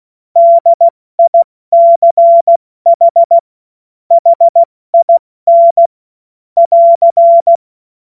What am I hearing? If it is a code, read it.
Morse code, heard as a keyed tone, about 16 wpm, spaced as for 14 wpm, 680 Hz. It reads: DICH HIN <AR>
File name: Morsekurs / Witze / Witz 24. August 2021, 70 F80